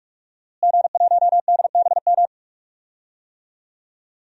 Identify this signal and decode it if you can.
Morse code, heard as a keyed tone, about 46 wpm, 700 Hz. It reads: G1DBM
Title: DXCC-100 Call Signs - 5 Characters - Repeated 3x - 45wpm